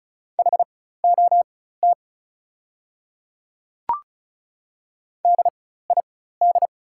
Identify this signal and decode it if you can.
Morse code, heard as a keyed tone, about 34 wpm, spaced as for 16 wpm, 700 Hz. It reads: HOT DID
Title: 4x Character Spacing - Top 100 Words - 35wpm